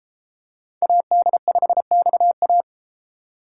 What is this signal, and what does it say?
Morse code, heard as a keyed tone, about 32 wpm, 700 Hz. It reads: AD5XA